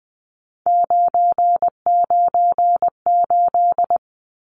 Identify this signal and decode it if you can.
Morse code, heard as a keyed tone, about 20 wpm, 700 Hz. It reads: 998
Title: Numbers - 3-Digit 20wpm